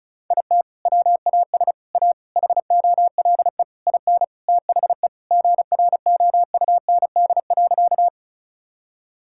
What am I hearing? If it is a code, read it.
Morse code, heard as a keyed tone, about 36 wpm, 700 Hz. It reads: IT WAS A HOLE IN THE GROUND.
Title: Sentences from Top 1600 Words 35wpm